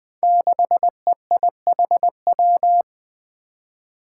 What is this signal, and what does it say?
Morse code, heard as a keyed tone, about 20 wpm, 700 Hz. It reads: 6EIHW
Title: Five Challenging Letter-Number Combinations 20wpm